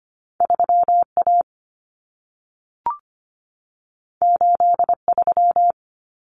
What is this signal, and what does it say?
Morse code, heard as a keyed tone, about 25 wpm, 700 Hz. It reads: 3A 83
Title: Common Cut Numbers - 2-Digit 25wpm